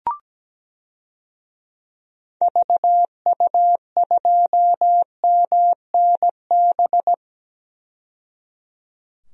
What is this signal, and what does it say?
Morse code, heard as a keyed tone, about 17 wpm, 700 Hz. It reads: VU2MNB
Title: DXCC-100 Call Signs - 6 Characters - 17wpm